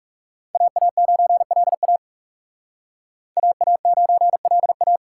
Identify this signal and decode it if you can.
Morse code, heard as a keyed tone, about 38 wpm, 700 Hz. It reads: AA9LA AA9LA